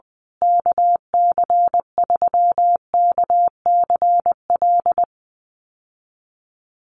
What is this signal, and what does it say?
Morse code, heard as a keyed tone, about 20 wpm, 700 Hz. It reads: KC3KCL
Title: USA Call Signs Repeated 3x 20wpm